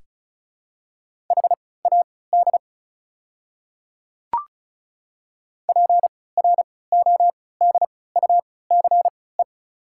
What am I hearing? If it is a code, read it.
Morse code, heard as a keyed tone, about 34 wpm, spaced as for 20 wpm, 700 Hz. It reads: HAD PRODUCE
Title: 3x Character Spacing - Top 500 Words - 35wpm